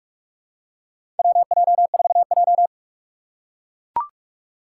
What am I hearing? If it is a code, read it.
Morse code, heard as a keyed tone, about 44 wpm, 700 Hz. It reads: WJ4J
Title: USA Call Signs - 4 Characters - 4x Character Spacing - 45wpm